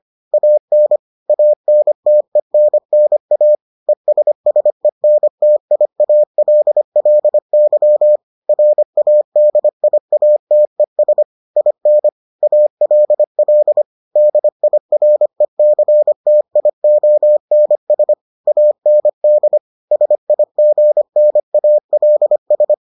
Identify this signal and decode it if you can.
Morse code, heard as a keyed tone, about 25 wpm, 600 Hz. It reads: AN ANTENNA ESSENTIALLY RADIATES IN ALL DIRECTIONS AND SIGNALS